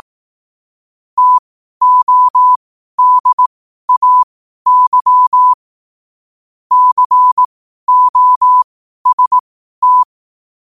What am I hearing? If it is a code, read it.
Morse code, heard as a keyed tone, about 18 wpm, spaced as for 12 wpm, 1000 Hz. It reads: TODAY COST
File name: qotdfiles / 12wpm